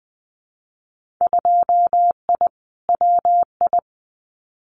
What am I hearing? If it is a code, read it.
Morse code, heard as a keyed tone, about 20 wpm, 700 Hz. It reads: 2I WI
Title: Winter Field Day Exchanges 20wpm